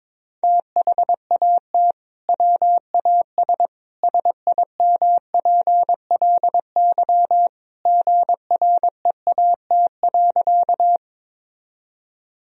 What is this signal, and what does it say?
Morse code, heard as a keyed tone, about 22 wpm, 700 Hz. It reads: THAT WAS SIMPLY GREAT.
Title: Sentences from Top 1000 Words - Review Part 4 - 22wpm